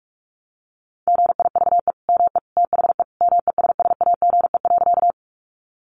Morse code, heard as a keyed tone, about 44 wpm, 700 Hz. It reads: GIVE ME THE MESSAGE.
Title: Sentences from Top 1100 Words 45wpm